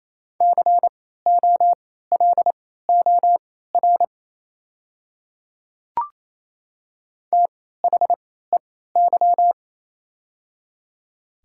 Morse code, heard as a keyed tone, about 28 wpm, spaced as for 16 wpm, 700 Hz. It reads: COLOR THEY
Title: 3x Character Spacing - Top 300 Words - 28wpm